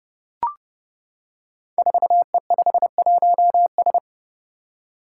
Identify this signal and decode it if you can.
Morse code, heard as a keyed tone, about 30 wpm, 700 Hz. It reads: 4E51S
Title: Five Challenging Letter-Number Combinations 30wpm